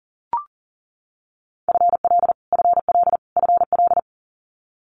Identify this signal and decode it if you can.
Morse code, heard as a keyed tone, about 41 wpm, 700 Hz. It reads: FL FL FL
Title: Multi-letter Phonemes Repeated 3x 40wpm